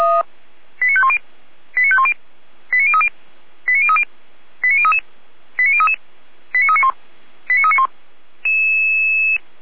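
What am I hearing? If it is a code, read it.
Five-tone selective call (ZVEI), heard as quick runs of five tones, tones of about 70 ms. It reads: 89510 89510 89030 89030 89033 89033 89391 89391
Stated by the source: ZVEI five-tone selective call, code 89050, received on an SDR.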